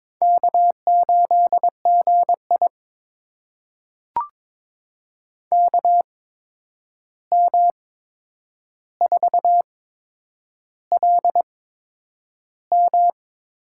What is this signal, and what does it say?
Morse code, heard as a keyed tone, about 21 wpm, spaced as for 9 wpm, 700 Hz. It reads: K8GI KM4LM
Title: USA Call Signs - 8x Character Spacing - 22wpm